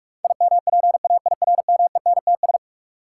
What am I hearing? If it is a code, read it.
Morse code, heard as a keyed tone, about 44 wpm, 700 Hz. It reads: IMPAIRMENTS
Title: Sets of 5 Words Encore - New Word Drill - 45wpm